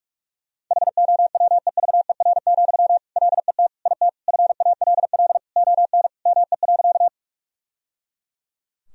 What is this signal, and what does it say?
Morse code, heard as a keyed tone, about 41 wpm, 700 Hz. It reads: HOWEVER, LET IT FALL ON ME.
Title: Sentences from Top 1000 Words - Review Part 2 - 45wpm